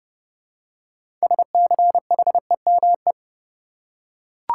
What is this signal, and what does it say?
Morse code, heard as a keyed tone, about 30 wpm, 700 Hz. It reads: SCHEME
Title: Six Letter Words - 30wpm